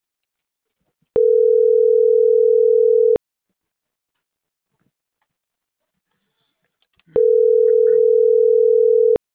Ringback tone (North America), 440 + 480 Hz.
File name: CENTRAL MAERSK / out alan.vicente